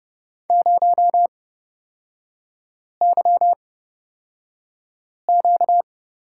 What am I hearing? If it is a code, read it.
Morse code, heard as a keyed tone, about 30 wpm, spaced as for 5 wpm, 700 Hz. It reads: 0YQ
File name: Lesson 38e - Letter q - Three Characters 30wpm